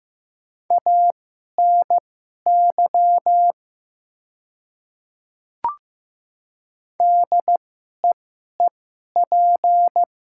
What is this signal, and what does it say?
Morse code, heard as a keyed tone, about 15 wpm, spaced as for 11 wpm, 700 Hz. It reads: ANY DEEP
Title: 2x Character Spacing - Top 500 Words - 15wpm